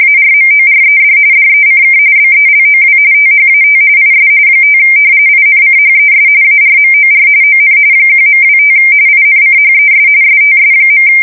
RTTY, about 45.45 baud. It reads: ARMORED DECK BELOW THAT IS 5 INCHES ARMOR ON 1.25 INCHES STS.
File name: fsk170 / battleshipnj